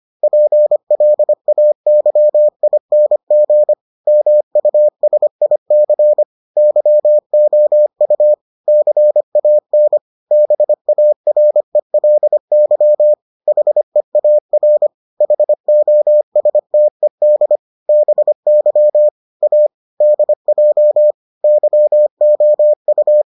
Morse code, heard as a keyed tone, about 25 wpm, 600 Hz. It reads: PLAYING MUSIC YOU CAN BARELY HEAR HOSTED BY A DJ YOU